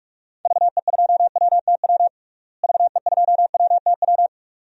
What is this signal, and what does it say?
Morse code, heard as a keyed tone, about 46 wpm, 700 Hz. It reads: VE2WTW VE2WTW
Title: DXCC-100 Call Signs - 6 Characters - Repeated 2x - 45wpm